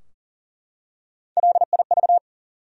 Morse code, heard as a keyed tone, about 40 wpm, 700 Hz. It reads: LIV
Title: USA Common Names - Max 3 Letters - 40wpm